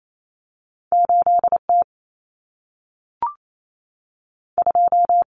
Morse code, heard as a keyed tone, about 28 wpm, 700 Hz. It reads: 8T 2
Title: Common Cut Numbers - 2-Digit 28wpm